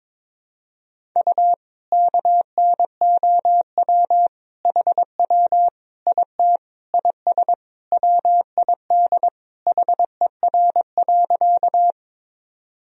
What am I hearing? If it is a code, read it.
Morse code, heard as a keyed tone, about 22 wpm, 700 Hz. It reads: U KNOW HW IT IS WID HER.
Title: Sentences from Top 100 Words and Abbreviations 22wpm